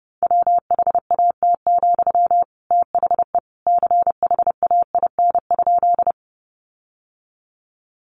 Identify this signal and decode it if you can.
Morse code, heard as a keyed tone, about 30 wpm, 700 Hz. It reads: WHAT, THE CHAIN?
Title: Sentences from Top 1500 Words 30wpm